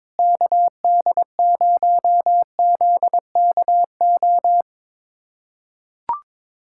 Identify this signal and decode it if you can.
Morse code, heard as a keyed tone, about 22 wpm, 700 Hz. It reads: KD0ZKO
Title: USA Callsign DE Callsign - 6 Characters - 22wpm